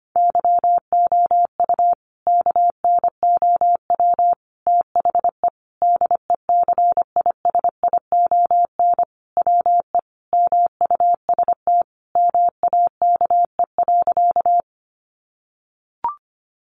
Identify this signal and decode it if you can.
Morse code, heard as a keyed tone, about 25 wpm, 700 Hz. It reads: YOU KNOW THE DECISION WE MUST MAKE.
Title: Sentences from Top 1000 Words 25wpm